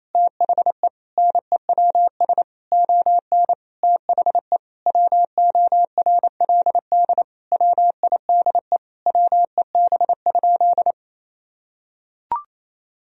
Morse code, heard as a keyed tone, about 28 wpm, 700 Hz. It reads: THE NEWS ON THE WORLD WIDE WEB?
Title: Sentences from Top 1000 Words - Review Part 4 - 28wpm